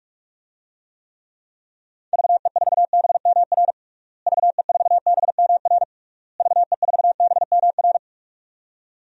Morse code, heard as a keyed tone, about 46 wpm, 700 Hz. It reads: VE4BMR VE4BMR VE4BMR
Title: DXCC-100 Call Signs - 6 Characters - Repeated 3x - 45wpm